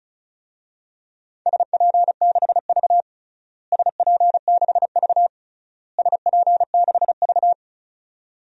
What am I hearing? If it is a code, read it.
Morse code, heard as a keyed tone, about 36 wpm, 690 Hz. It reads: SP6V SP6V SP6V
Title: DXCC-100 Call Signs - 4 Characters - Repeated 3x - 35wpm